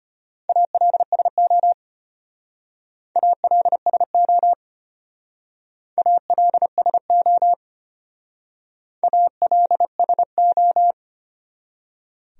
Morse code, heard as a keyed tone, about 31 wpm, 700 Hz. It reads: ALSO ALSO ALSO ALSO